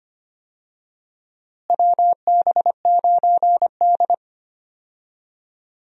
Morse code, heard as a keyed tone, about 25 wpm, 700 Hz. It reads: WB9D
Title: USA Call Signs - 4 Characters - Repeated 2x - 25wpm